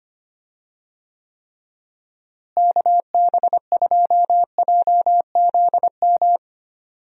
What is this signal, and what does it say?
Morse code, heard as a keyed tone, about 25 wpm, 700 Hz. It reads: KB2JZM
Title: USA Call Signs - 6 Characters - Repeated 3x - 25wpm